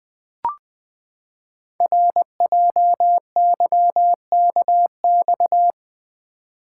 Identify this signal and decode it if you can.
Morse code, heard as a keyed tone, about 20 wpm, 700 Hz. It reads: RJYKX
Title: Code Groups - 5 Letters - 20wpm